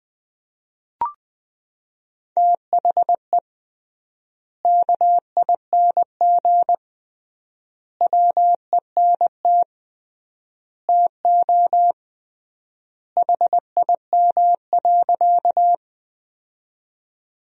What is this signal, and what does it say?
Morse code, heard as a keyed tone, about 20 wpm, 700 Hz. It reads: THE KING WENT TO HIM.
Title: 3x Word-Spacing - Sentences from Top 400 Words - 20wpm